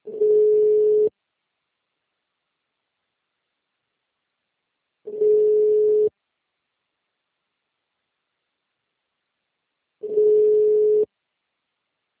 Ringback tone (Europe, 425 Hz).